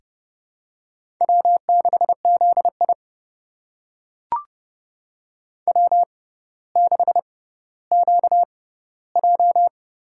Morse code, heard as a keyed tone, about 29 wpm, spaced as for 15 wpm, 700 Hz. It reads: W6ZI W6QJ
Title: USA Call Signs - 4 Characters - 6x Character Spacing - 30wpm